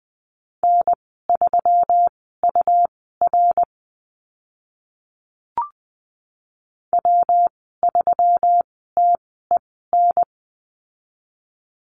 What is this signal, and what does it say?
Morse code, heard as a keyed tone, about 20 wpm, spaced as for 15 wpm, 700 Hz. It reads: N3UR W3TEN
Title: 140 Lesson 23g - Letter f - Call Signs 20wpm